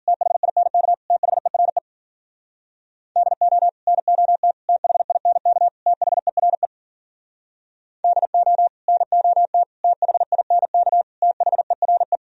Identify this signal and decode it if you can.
Morse code, heard as a keyed tone, about 51 wpm, 700 Hz. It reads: THINK THERE DO NOT THINK THERE DO NOT THINK THERE